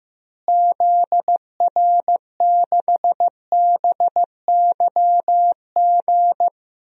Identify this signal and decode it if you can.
Morse code, heard as a keyed tone, about 15 wpm, 700 Hz. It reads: ZR6BYG
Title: DXCC-100 Call Signs - 6 Characters - Repeated 2x - 15wpm